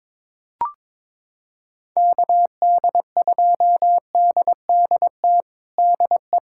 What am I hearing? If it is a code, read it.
Morse code, heard as a keyed tone, about 22 wpm, 700 Hz. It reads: KD2DDT DE
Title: USA Callsign DE Callsign - 6 Characters - 22wpm